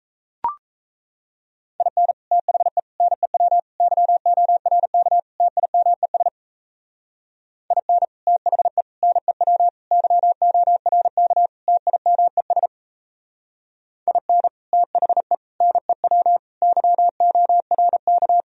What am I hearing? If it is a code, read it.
Morse code, heard as a keyed tone, about 35 wpm, 700 Hz. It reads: IN THE NEW YORK TIMES IN THE NEW YORK TIMES IN THE NEW YORK